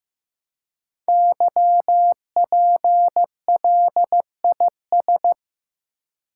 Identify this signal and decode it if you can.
Morse code, heard as a keyed tone, about 15 wpm, 700 Hz. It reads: YPLIS
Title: Code Groups - 5 Letters - 15wpm